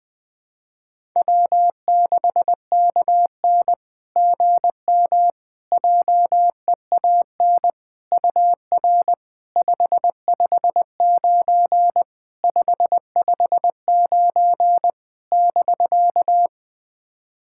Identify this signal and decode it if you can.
Morse code, heard as a keyed tone, about 21 wpm, 690 Hz. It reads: W6KN GM JEAN UR 559 559 <BK>
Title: SOTA QSO Practice 20wpm